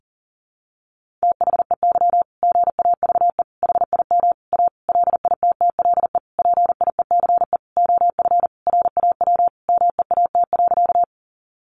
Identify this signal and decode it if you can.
Morse code, heard as a keyed tone, about 40 wpm, 700 Hz. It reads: THEY GAVE HIM A LITTLE PIECE OF RAW MEAT.
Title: Sentences from Top 1500 Words 40wpm